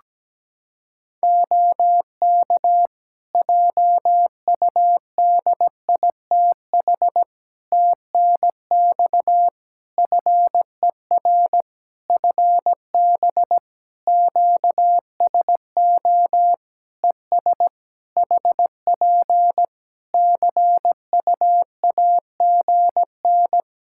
Morse code, heard as a keyed tone, about 17 wpm, 700 Hz. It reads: OK JUDITH TNX FER FB QSO ES HP CUAGN
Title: Practice QSO - Exchange 4 - Ending - 17wpm